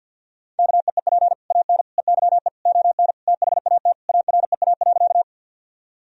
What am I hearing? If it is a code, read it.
Morse code, heard as a keyed tone, about 50 wpm, 700 Hz. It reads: KEEP AN EYE ON THAT AREA.
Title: Sentences from Top 1000 Words - Review Part 2 - 50wpm